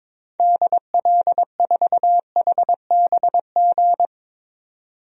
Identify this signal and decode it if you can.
Morse code, heard as a keyed tone, about 22 wpm, 700 Hz. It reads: DL4HBG
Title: DXCC-100 Call Signs - 6 Characters - 22wpm